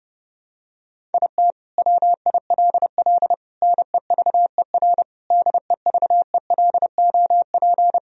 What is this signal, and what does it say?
Morse code, heard as a keyed tone, about 32 wpm, 700 Hz. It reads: IT WILL NEVER DEVELOP
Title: Sentences from Top 1000 Words - Review Part 2 - 30wpm